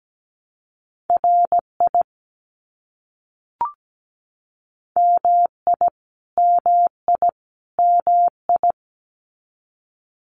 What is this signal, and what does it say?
Morse code, heard as a keyed tone, about 17 wpm, 700 Hz. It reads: RI MI MI MI
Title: US States Repeated 3x 17wpm